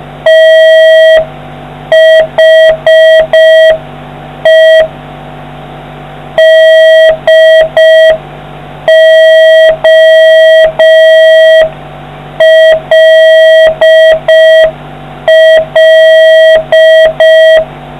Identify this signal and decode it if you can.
Morse code, heard as a keyed tone, about 5 wpm, 1950 Hz, with noise. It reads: THE DOLL